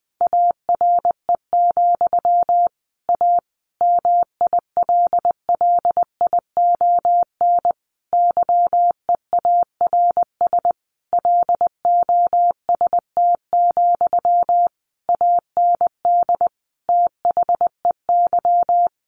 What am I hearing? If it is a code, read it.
Morse code, heard as a keyed tone, about 20 wpm, 700 Hz. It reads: ARE, A MILLION YEARS LOST, AND THEY
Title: A Sound Of Thunder 20wpm